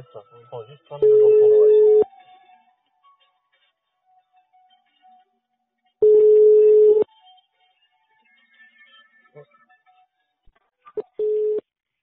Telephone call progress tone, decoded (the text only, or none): ringback tone